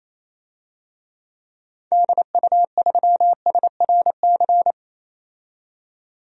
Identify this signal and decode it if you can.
Morse code, heard as a keyed tone, about 28 wpm, 700 Hz. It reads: DU3SRC
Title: DXCC-100 Call Signs - 6 Characters - 28wpm